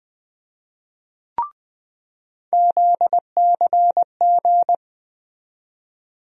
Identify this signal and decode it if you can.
Morse code, heard as a keyed tone, about 20 wpm, 700 Hz. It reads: ZCG
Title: USA Call Sign Postfixes - 3 Letters 20wpm